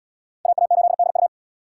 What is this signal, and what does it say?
Morse code, heard as a keyed tone, about 66 wpm, 700 Hz. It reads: AA8DU